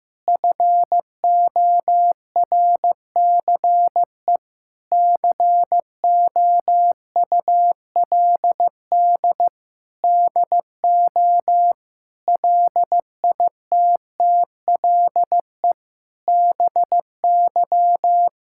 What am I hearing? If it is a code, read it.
Morse code, heard as a keyed tone, about 15 wpm, 700 Hz. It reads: FORCE COULD DO LITTLE BY
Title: Sentences from Top 1000 Words - Review Part 5 - 15wpm